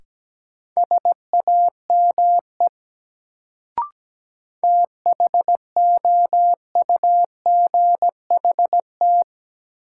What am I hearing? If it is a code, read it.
Morse code, heard as a keyed tone, about 17 wpm, 700 Hz. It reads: SAME THOUGHT
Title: Sets of 2 Words - New Word Drill - 17wpm